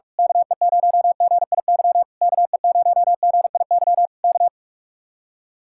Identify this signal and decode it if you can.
Morse code, heard as a keyed tone, about 44 wpm, 700 Hz. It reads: KE0GIY KE0GIY K
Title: Speed-Racing Common QSO Phrases - 45wpm